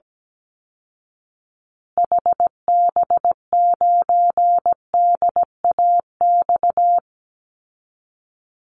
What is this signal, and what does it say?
Morse code, heard as a keyed tone, about 17 wpm, 700 Hz. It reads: HB9DAX
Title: Call Signs Repeated 2x - 17wpm